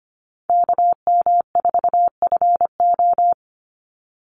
Morse code, heard as a keyed tone, about 25 wpm, 700 Hz. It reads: KM4FO
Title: Call Signs Repeated 3x - 25wpm